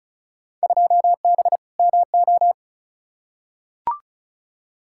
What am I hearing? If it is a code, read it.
Morse code, heard as a keyed tone, about 36 wpm, 700 Hz. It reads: 2B MO
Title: Field Day Exchanges - 35wpm